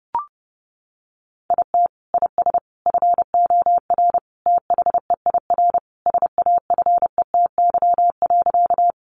Morse code, heard as a keyed tone, about 30 wpm, 700 Hz. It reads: IT IS FOR THEIR SAFETY.
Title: Sentences from Top 700 Words 30wpm